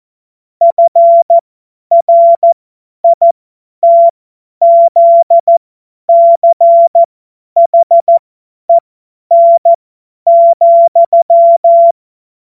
Morse code, heard as a keyed tone, about 14 wpm, spaced as for 10 wpm, 680 Hz. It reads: FRITZCHEN,
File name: Morsekurs / Witze / Witz 13. September 2022 50 F70